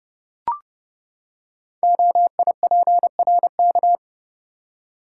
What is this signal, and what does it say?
Morse code, heard as a keyed tone, about 28 wpm, 700 Hz. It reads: OIPRK
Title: Code Groups - 5 Letters - 30wpm